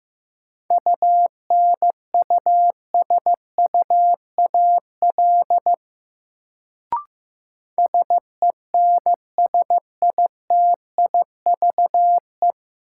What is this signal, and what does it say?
Morse code, heard as a keyed tone, about 15 wpm, 700 Hz. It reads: UNUSUAL SENSITIVE
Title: Sentences from Top 1400 Words - New Word Drill - 15wpm